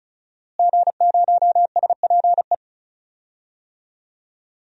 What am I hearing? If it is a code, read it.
Morse code, heard as a keyed tone, about 36 wpm, 700 Hz. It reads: G0SPE
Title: DXCC-100 Call Signs - 5 Characters - Repeated 3x - 35wpm